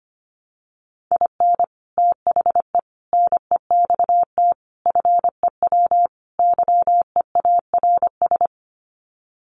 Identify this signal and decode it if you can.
Morse code, heard as a keyed tone, about 25 wpm, 700 Hz. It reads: IN THE NEXT FEW YEARS
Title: Sets of 5 Words Encore 25wpm